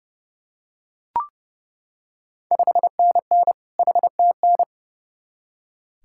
Morse code, heard as a keyed tone, about 30 wpm, 690 Hz. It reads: 5NN HTN
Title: Ontario QSO Party Exchange 30wpm